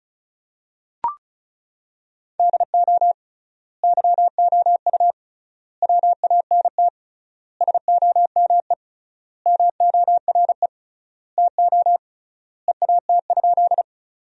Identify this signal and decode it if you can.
Morse code, heard as a keyed tone, about 36 wpm, 700 Hz. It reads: DO YOU WANT SOME MORE TO EAT?
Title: 3x Word-Spacing - Sentences from Top 300 Words - 35wpm